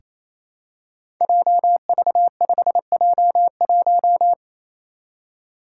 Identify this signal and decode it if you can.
Morse code, heard as a keyed tone, about 28 wpm, 700 Hz. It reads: JV5J1
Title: Five Challenging Letter-Number Combinations 28wpm